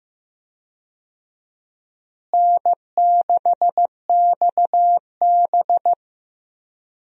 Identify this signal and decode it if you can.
Morse code, heard as a keyed tone, about 15 wpm, 700 Hz. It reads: N6XB
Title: USA Call Signs - 4 Characters - Repeated 2x - 15wpm